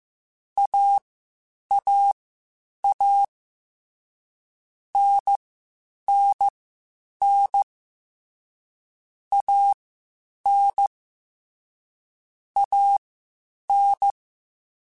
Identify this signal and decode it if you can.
Morse code, heard as a keyed tone, about 15 wpm, spaced as for 8 wpm, 800 Hz. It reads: AAA NNN AN AN
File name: Short Runs / Short Runs-mp3s / Unballanced @13